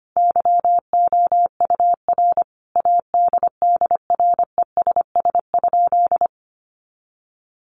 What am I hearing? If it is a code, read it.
Morse code, heard as a keyed tone, about 25 wpm, 700 Hz. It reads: YOUR ADDRESS?
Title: Sentences from Top 1000 Words 25wpm